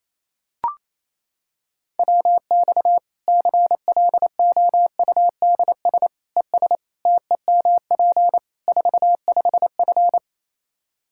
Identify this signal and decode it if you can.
Morse code, heard as a keyed tone, about 28 wpm, 700 Hz. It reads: WX CLOUDS ES TEMP 45F
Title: Practice QSO - Exchange 2 - Info - 28wpm